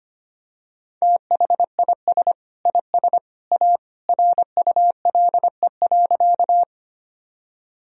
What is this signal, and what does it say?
Morse code, heard as a keyed tone, about 25 wpm, 700 Hz. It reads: THIS IS A RULE.
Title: Sentences from Top 1000 Words - Review Part 2 - 25wpm